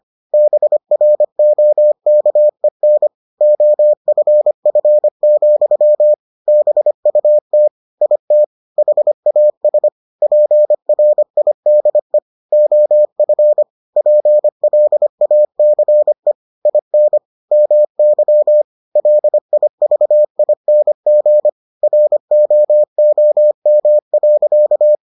Morse code, heard as a keyed tone, about 25 wpm, 600 Hz. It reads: BROKEN OFF, BUT IT HAS PRIDE OF PLACE IN MY LIVING ROOM.